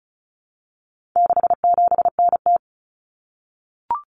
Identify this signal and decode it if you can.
Morse code, heard as a keyed tone, about 34 wpm, 700 Hz. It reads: 67NT
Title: Common Cut Numbers - 4-Digit 35wpm